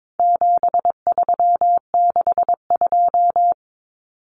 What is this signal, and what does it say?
Morse code, heard as a keyed tone, about 22 wpm, 700 Hz. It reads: 7362
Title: Numbers - 4-Digit 22wpm